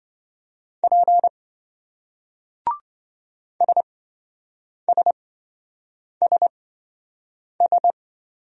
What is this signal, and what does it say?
Morse code, heard as a keyed tone, about 28 wpm, spaced as for 7 wpm, 700 Hz. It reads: P SSSS